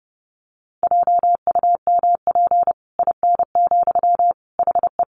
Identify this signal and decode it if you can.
Morse code, heard as a keyed tone, about 30 wpm, 700 Hz. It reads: JUMP IN, HE
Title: Sentences from Top 1300 Words 30wpm